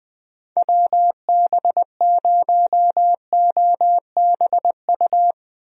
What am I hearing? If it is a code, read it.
Morse code, heard as a keyed tone, about 20 wpm, 700 Hz. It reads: WB0OBU
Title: USA Call Signs - 6 Characters - Repeated 3x - 20wpm